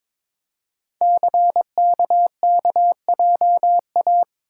Morse code, heard as a keyed tone, about 22 wpm, 700 Hz. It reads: CKKJA
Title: Code Groups - 5 Letters - 22wpm